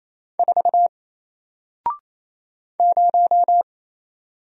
Morse code, heard as a keyed tone, about 28 wpm, 700 Hz. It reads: 4 0
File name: Single-Letter-Number-28wpm